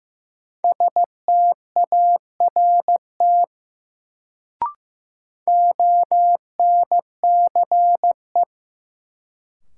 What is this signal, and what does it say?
Morse code, heard as a keyed tone, about 16 wpm, 700 Hz. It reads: START ONCE
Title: Sentences from Top 300 Words - New Word Drill - 15wpm